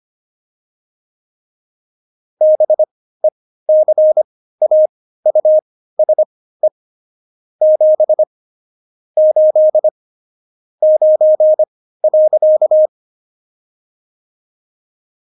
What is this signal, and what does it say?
Morse code, heard as a keyed tone, about 25 wpm, spaced as for 15 wpm, 620 Hz. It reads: BECAUSE 7 8 9.
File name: Jokes-25-char-15-eff